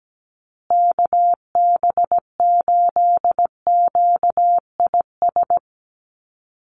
Morse code, heard as a keyed tone, about 17 wpm, 700 Hz. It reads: KB8QIS
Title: USA Call Signs Repeated 3x 17wpm